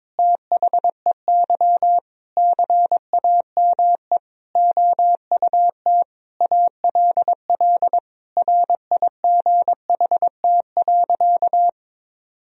Morse code, heard as a keyed tone, about 22 wpm, 700 Hz. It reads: THEY CAME OUT ALL RIGHT.
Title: Sentences from Top 1000 Words - Review Part 4 - 22wpm